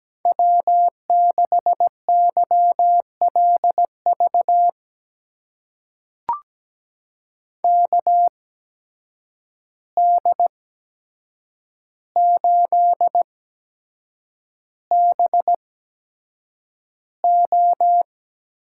Morse code, heard as a keyed tone, about 17 wpm, spaced as for 7 wpm, 700 Hz. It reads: W6YLV KD8BO
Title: USA Call Signs - 8x Character Spacing - 17wpm